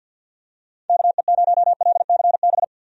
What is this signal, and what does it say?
Morse code, heard as a keyed tone, about 52 wpm, 700 Hz. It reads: KE0LCB
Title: USA Call Signs - 6 Characters - Repeated 3x - 50wpm